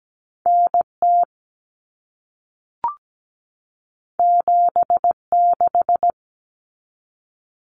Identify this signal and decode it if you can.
Morse code, heard as a keyed tone, about 17 wpm, 700 Hz. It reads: NT 76
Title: Common Cut Numbers - 2-Digit 17wpm